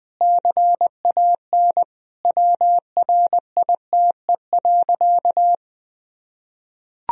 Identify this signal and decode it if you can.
Morse code, heard as a keyed tone, about 20 wpm, 700 Hz. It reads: CAN WRITE.
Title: Sentences from Top 1000 Words - Review Part 1 - 20wpm